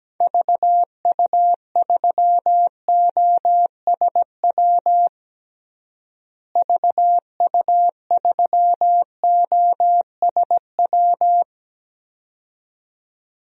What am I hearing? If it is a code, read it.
Morse code, heard as a keyed tone, about 17 wpm, 700 Hz. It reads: VU3OSW VU3OSW
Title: DXCC-100 Call Signs - 6 Characters - Repeated 2x - 17wpm